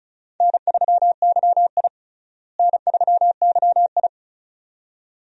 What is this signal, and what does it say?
Morse code, heard as a keyed tone, about 36 wpm, 700 Hz. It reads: N3YI N3YI
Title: USA Call Signs - 4 Characters - Repeated 2x - 35wpm